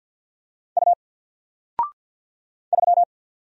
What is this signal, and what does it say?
Morse code, heard as a keyed tone, about 48 wpm, 700 Hz. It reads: U 3
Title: Single Letter-Number Alt 50wpm